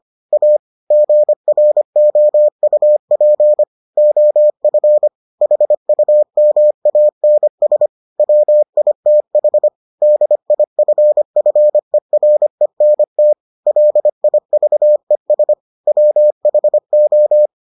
Morse code, heard as a keyed tone, about 25 wpm, 600 Hz. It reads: A GROUP OF HUMANS WITH DIFFERENT LIVES WHO